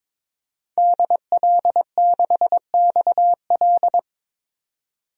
Morse code, heard as a keyed tone, about 21 wpm, 700 Hz. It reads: DL6XL
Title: DXCC-100 Call Signs - 5 Characters - 22wpm